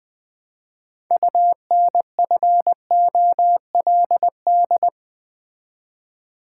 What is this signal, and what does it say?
Morse code, heard as a keyed tone, about 20 wpm, 700 Hz. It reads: UNFOLD
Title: Build A Word - 3 Letters At A Time - 6-Letter Words - 20wpm